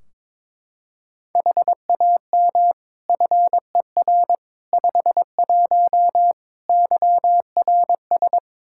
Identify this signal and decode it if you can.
Morse code, heard as a keyed tone, about 22 wpm, 700 Hz. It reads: HAM FER 51 YRS
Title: Practice QSO - Exchange 3 - More Info - 22wpm